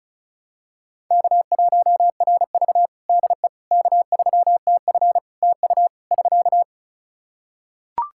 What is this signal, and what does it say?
Morse code, heard as a keyed tone, about 36 wpm, 700 Hz. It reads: K1RV DE K3TF TU <SK>
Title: Practice QSO - Exchange 4 - Ending - 35wpm